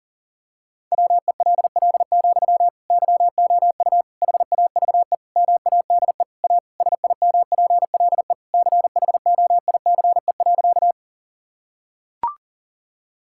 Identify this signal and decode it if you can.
Morse code, heard as a keyed tone, about 40 wpm, 700 Hz. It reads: WELL, YOU HAVE MADE A SIMPLE CHOICE.
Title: Sentences from Top 1000 Words - Review Part 6 - 40wpm